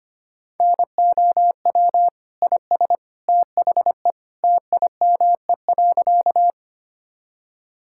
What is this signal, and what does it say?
Morse code, heard as a keyed tone, about 25 wpm, 700 Hz. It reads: NOW IS THE TIME.
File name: practice / 25wpm / Sentences with 2-Word Phrases - 25wpm